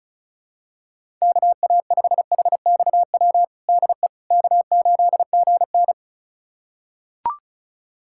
Morse code, heard as a keyed tone, about 36 wpm, 700 Hz. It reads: KA5HXW DE K8GN
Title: USA Callsign DE Callsign 35wpm